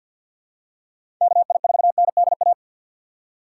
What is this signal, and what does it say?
Morse code, heard as a keyed tone, about 52 wpm, 700 Hz. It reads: KI4NDA
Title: USA Call Signs - 3x Character Spacing - 50wpm